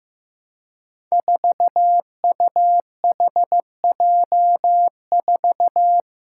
Morse code, heard as a keyed tone, about 15 wpm, 700 Hz. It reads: 4UHJ4
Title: Five Challenging Letter-Number Combinations 15wpm